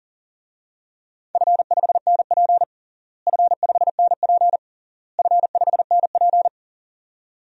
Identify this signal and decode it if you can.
Morse code, heard as a keyed tone, about 40 wpm, 700 Hz. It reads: F5NP F5NP F5NP
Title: DXCC-100 Call Signs - 4 Characters - Repeated 3x - 40wpm